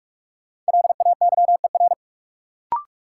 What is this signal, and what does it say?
Morse code, heard as a keyed tone, about 47 wpm, 700 Hz. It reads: LAYER
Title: Sentences from Top 1600 Words - New Word Drill - 45wpm